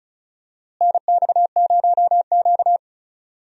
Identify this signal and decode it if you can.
Morse code, heard as a keyed tone, about 34 wpm, 700 Hz. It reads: NX0Q
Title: USA Call Signs - 4 Characters - Repeated 3x - 35wpm